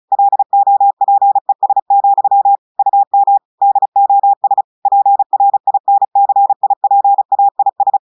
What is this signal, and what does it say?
Morse code, heard as a keyed tone, about 36 wpm, 800 Hz. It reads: LOPES, UM DOS PRINCIPAIS
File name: audios / portugais / portugais texte en clair 1 vitesse 35 mots minutes